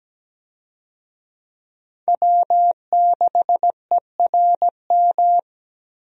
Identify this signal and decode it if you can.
Morse code, heard as a keyed tone, about 17 wpm, 700 Hz. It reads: W6ERM
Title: USA Call Signs - 5 Characters - 17wpm